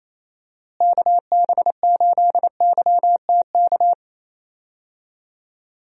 Morse code, heard as a keyed tone, about 28 wpm, 700 Hz. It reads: KB8YTK
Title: USA Call Signs - 6 Characters - Repeated 2x - 28wpm